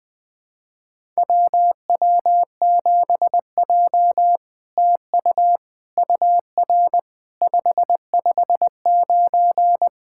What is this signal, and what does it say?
Morse code, heard as a keyed tone, about 20 wpm, 700 Hz. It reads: WW7J TU UR 559
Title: SOTA QSO Practice 20wpm